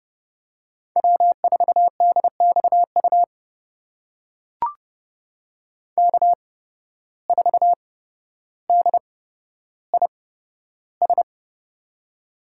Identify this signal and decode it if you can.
Morse code, heard as a keyed tone, about 30 wpm, spaced as for 13 wpm, 700 Hz. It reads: W4DXU K4DIS
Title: USA Call Signs - 5 Characters - 8x Character Spacing - 30wpm